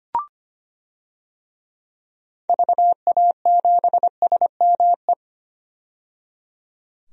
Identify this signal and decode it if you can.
Morse code, heard as a keyed tone, about 25 wpm, 700 Hz. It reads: VA7SME
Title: DXCC-100 Call Signs - 6 Characters - 25wpm